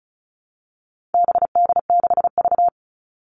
Morse code, heard as a keyed tone, about 34 wpm, 700 Hz. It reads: BD6V
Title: All Cut Numbers - 4-Digit 35wpm